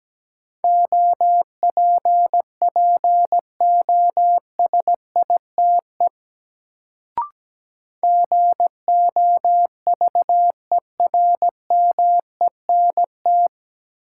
Morse code, heard as a keyed tone, about 17 wpm, 700 Hz. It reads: OPPOSITE GOVERMENT
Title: Sentences from Top 1300 Words - New Word Drill - 17wpm